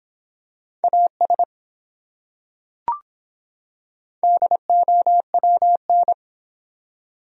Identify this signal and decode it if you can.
Morse code, heard as a keyed tone, about 24 wpm, 700 Hz. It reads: AS DOWN